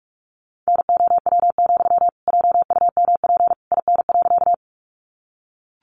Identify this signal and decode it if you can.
Morse code, heard as a keyed tone, about 44 wpm, 700 Hz. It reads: NOW, JUMP IN.
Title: Sentences from Top 1300 Words 45wpm